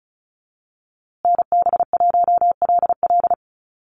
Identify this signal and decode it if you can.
Morse code, heard as a keyed tone, about 34 wpm, 700 Hz. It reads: NB1LL
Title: USA Call Signs Repeated 3x 35wpm